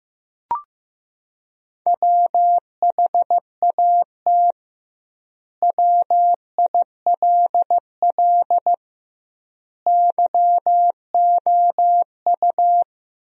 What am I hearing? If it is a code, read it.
Morse code, heard as a keyed tone, about 15 wpm, 700 Hz. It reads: WHAT WILL YOU
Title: 2x Word-Spacing - Sentences from Top 100 Words - 15wpm